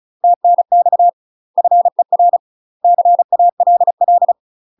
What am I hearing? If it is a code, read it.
Morse code, heard as a keyed tone, about 35 wpm, 700 Hz. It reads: TNX FER CALL
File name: UK MASTER QSOs TRAINING / 35WPM / UK MASTER QSOs 0022